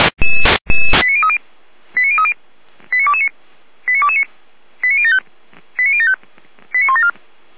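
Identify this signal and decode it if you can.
Five-tone selective call (ZVEI), heard as quick runs of five tones, tones of about 70 ms. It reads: 89030 89030 89229 89229 89075 89075 89174
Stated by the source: ZVEI five-tone selective call 89250 received on an SDR, served by feuerwehr-orlamuende (Orlamünde fire brigade).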